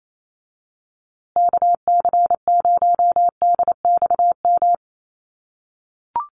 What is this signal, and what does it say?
Morse code, heard as a keyed tone, about 28 wpm, 700 Hz. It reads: KC0DXM
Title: USA Call Signs Repeated 2x 28wpm